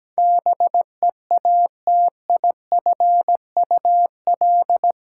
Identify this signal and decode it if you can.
Morse code, heard as a keyed tone, about 17 wpm, 700 Hz. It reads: BEATIFUL
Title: Sentences from Top 1300 Words - New Word Drill - 17wpm